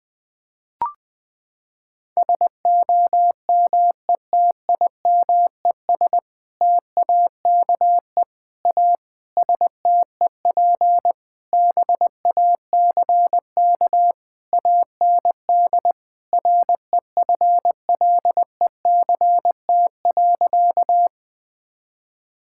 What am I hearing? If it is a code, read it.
Morse code, heard as a keyed tone, about 20 wpm, 700 Hz. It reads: SOMETIMES TAKE A STEP BACK AND REFLECT.
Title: Sentences with 6-Word Phrases - 20wpm